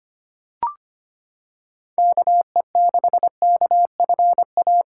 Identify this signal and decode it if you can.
Morse code, heard as a keyed tone, about 25 wpm, 700 Hz. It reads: KE6KFA
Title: USA Callsign DE Callsign - 6 Characters - 25wpm